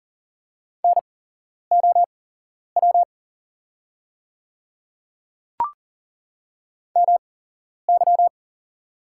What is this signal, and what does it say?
Morse code, heard as a keyed tone, about 40 wpm, spaced as for 11 wpm, 690 Hz. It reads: NOW MY
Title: 8x Character Spacing - Top 100 Words - 40wpm